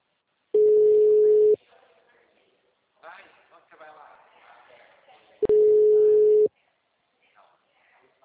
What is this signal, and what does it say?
Ringback tone (Europe, 425 Hz), with speech.
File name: out- -2006- - - .87428